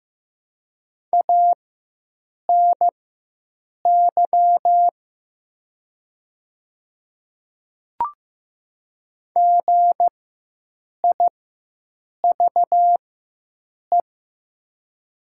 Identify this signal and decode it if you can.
Morse code, heard as a keyed tone, about 15 wpm, spaced as for 7 wpm, 700 Hz. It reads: ANY GIVE
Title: 4x Character Spacing - Top 500 Words - 15wpm